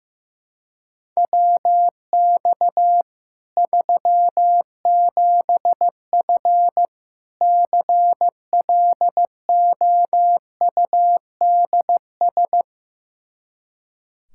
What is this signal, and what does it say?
Morse code, heard as a keyed tone, about 15 wpm, 700 Hz. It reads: WX 37F CLOUDS
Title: Practice QSO - Exchange 2 - Info - 15wpm